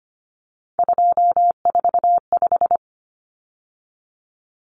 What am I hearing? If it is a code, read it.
Morse code, heard as a keyed tone, about 25 wpm, 700 Hz. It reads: 245
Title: Common Cut Numbers - 3-Digit 25wpm